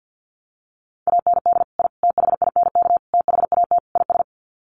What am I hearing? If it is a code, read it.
Morse code, heard as a keyed tone, about 50 wpm, 700 Hz. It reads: AND I THINK THAT IS
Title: Sets of 5 Words Encore2 50wpm